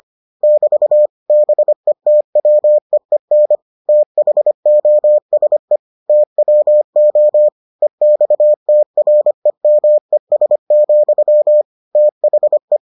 Morse code, heard as a keyed tone, about 25 wpm, 600 Hz. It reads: <BT> BETWEEN THOSE TWO EXTREMES, THE